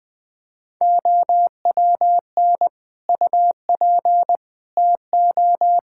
Morse code, heard as a keyed tone, about 20 wpm, 700 Hz. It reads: OWN UP TO
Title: Difficult Sentences with Hints - 20wpm